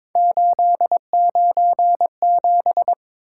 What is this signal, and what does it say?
Morse code, heard as a keyed tone, about 22 wpm, 700 Hz. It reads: 897
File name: practice / 22wpm / Three Letter-Number Combinations - 22wpm